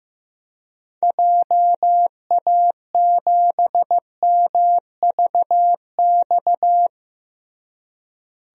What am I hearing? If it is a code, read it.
Morse code, heard as a keyed tone, about 15 wpm, 700 Hz. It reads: JA7MVX
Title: DXCC-100 Call Signs - 6 Characters - Repeated 3x - 15wpm